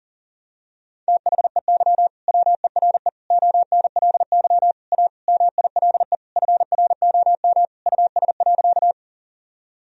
Morse code, heard as a keyed tone, about 40 wpm, 700 Hz. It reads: THEY WERE ONLY A MILE FROM US.
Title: Sentences from Top 1000 Words - Review Part 1 - 40wpm